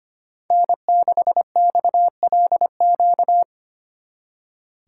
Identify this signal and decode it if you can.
Morse code, heard as a keyed tone, about 30 wpm, 700 Hz. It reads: N6XLQ
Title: Speed Racing - USA 1x3 Call Signs 25wpm